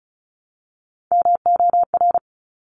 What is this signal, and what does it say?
Morse code, heard as a keyed tone, about 36 wpm, 700 Hz. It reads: MOR
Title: Top Three Letter Combinations in Words 35wpm